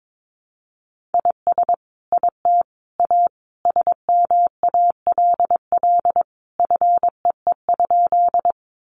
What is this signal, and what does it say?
Morse code, heard as a keyed tone, about 22 wpm, 700 Hz. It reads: IS IT A SMALL FEE?
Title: Sentences from Top 1500 Words 22wpm